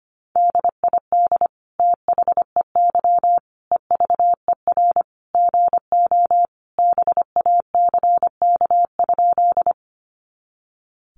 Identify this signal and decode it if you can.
Morse code, heard as a keyed tone, about 25 wpm, 700 Hz. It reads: DID THEY EVER GO BACK?
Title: Sentences from Top 400 Words 25wpm